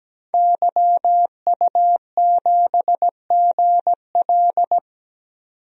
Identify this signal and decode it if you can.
Morse code, heard as a keyed tone, about 17 wpm, 700 Hz. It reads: YU7GL
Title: DXCC-100 Call Signs - 5 Characters - 17wpm